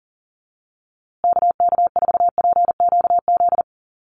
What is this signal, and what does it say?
Morse code, heard as a keyed tone, about 37 wpm, 700 Hz. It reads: KK4PQZ
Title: USA Call Signs Repeated 2x 40wpm